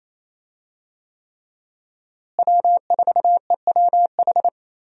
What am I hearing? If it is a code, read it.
Morse code, heard as a keyed tone, about 28 wpm, 700 Hz. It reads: W4EWH
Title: USA Call Signs - 5 Characters - Repeated 3x - 28wpm